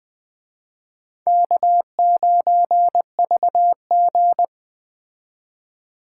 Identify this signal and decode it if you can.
Morse code, heard as a keyed tone, about 20 wpm, 700 Hz. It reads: K9VG
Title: Speed Racing - USA 1x2 Call Signs 17wpm